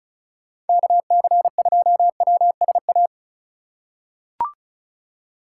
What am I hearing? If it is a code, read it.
Morse code, heard as a keyed tone, about 34 wpm, 700 Hz. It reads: KC2WSA
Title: USA Call Signs - 6 Characters - 3x Character Spacing - 35wpm